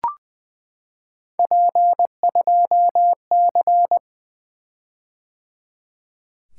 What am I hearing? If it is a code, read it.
Morse code, heard as a keyed tone, about 20 wpm, 700 Hz. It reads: P2C
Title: Three Letter-Number Combinations - 20wpm